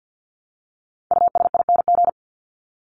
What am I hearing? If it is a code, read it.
Morse code, heard as a keyed tone, about 50 wpm, 700 Hz. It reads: USING
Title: ING Suffix - Top 100 Words with Base Words 50wpm